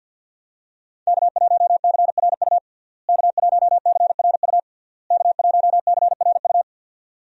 Morse code, heard as a keyed tone, about 52 wpm, 700 Hz. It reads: K1CRU K1CRU K1CRU
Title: USA Call Signs - 5 Characters - Repeated 3x - 50wpm